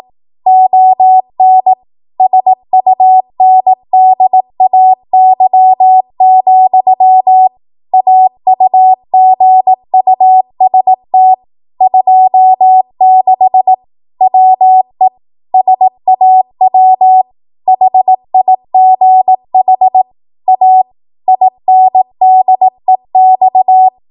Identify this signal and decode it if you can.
Morse code, heard as a keyed tone, about 18 wpm, 750 Hz. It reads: ON SUNDAY, AUGUST 26 WE SAW HIGH A INDEX
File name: Archive / 18 WPM / 18WPM